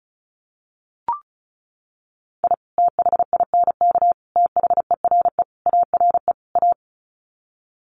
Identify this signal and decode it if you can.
Morse code, heard as a keyed tone, about 34 wpm, 700 Hz. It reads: I THINK THERE ARE A
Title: Sets of 5 Words Encore2 35wpm